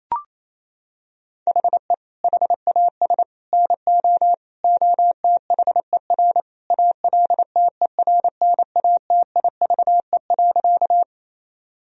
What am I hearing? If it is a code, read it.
Morse code, heard as a keyed tone, about 27 wpm, 700 Hz. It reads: HE HAS NO OTHER ALTERNATIVE.
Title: Sentences from Top 1000 Words - Review Part 4 - 28wpm